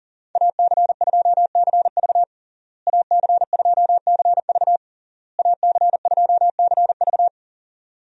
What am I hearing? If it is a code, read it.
Morse code, heard as a keyed tone, about 40 wpm, 700 Hz. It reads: AC2CV AC2CV AC2CV
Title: USA Call Signs - 5 Characters - Repeated 3x - 40wpm